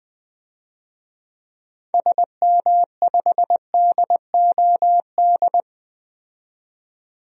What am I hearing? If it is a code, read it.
Morse code, heard as a keyed tone, about 20 wpm, 700 Hz. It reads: SM5DOD
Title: DXCC-100 Call Signs - 6 Characters - 20wpm